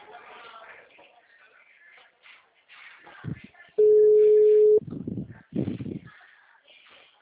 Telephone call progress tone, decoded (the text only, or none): ringback tone